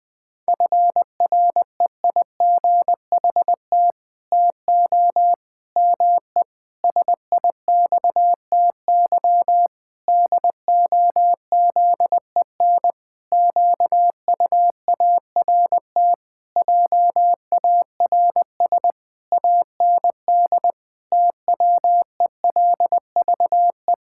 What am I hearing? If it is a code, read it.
Morse code, heard as a keyed tone, about 20 wpm, 700 Hz. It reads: FREIGHT TO ME SIXTY DOZEN QUART JARS AND TWELVE
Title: Pangram Challenge v2 - 20wpm